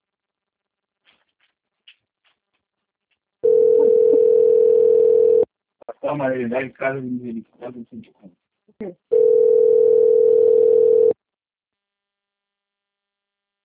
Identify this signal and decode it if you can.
Ringback tone (North America), 440 + 480 Hz, with speech.
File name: rg-1600-unknown- - - .78167